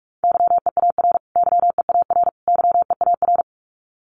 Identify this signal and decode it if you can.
Morse code, heard as a keyed tone, about 44 wpm, 700 Hz. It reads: YEAR YEAR YEAR
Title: Top 200 Words Repeated 45wpm